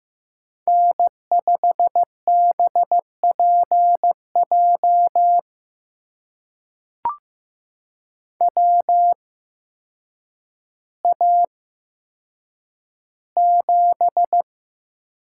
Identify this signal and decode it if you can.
Morse code, heard as a keyed tone, about 15 wpm, spaced as for 8 wpm, 700 Hz. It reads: N5BPJ WA7
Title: USA Call Signs - 5 Characters - 8x Character Spacing - 15wpm